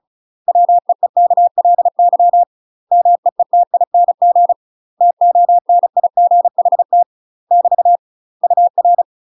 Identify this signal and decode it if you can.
Morse code, heard as a keyed tone, about 34 wpm, 700 Hz. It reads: WEEKLY MEETING TONIGHT <BT> UR
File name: UK MASTER QSOs TRAINING / 35WPM / UK MASTER QSOs 0030